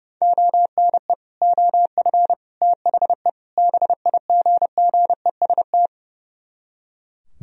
Morse code, heard as a keyed tone, about 36 wpm, 700 Hz. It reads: ONE OF THE BIGGEST
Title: Speed-Racing - Sets of 4 Words 30wpm